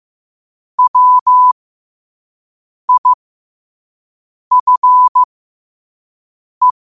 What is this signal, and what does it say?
Morse code, heard as a keyed tone, about 15 wpm, spaced as for 5 wpm, 1000 Hz. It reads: WIFE
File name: qotdfiles / 5wpm